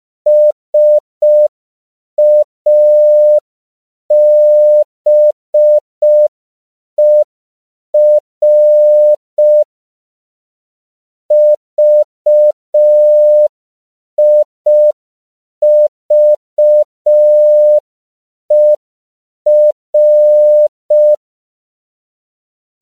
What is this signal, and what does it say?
Morse code, heard as a keyed tone, about 5 wpm, 600 Hz. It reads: SABER VIVER
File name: CW / 5PPM